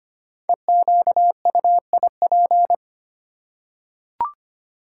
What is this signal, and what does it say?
Morse code, heard as a keyed tone, about 25 wpm, 700 Hz. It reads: EQUIP
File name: 5 Letter Words - First 3 Letters Given - 25wpm